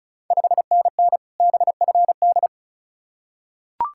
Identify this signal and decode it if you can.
Morse code, heard as a keyed tone, about 36 wpm, 690 Hz. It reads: 5NN BFD
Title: Ontario QSO Party Exchange 35wpm